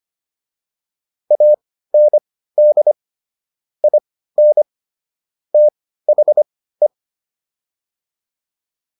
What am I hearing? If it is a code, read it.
Morse code, heard as a keyed tone, about 25 wpm, spaced as for 15 wpm, 600 Hz. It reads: AND IN THE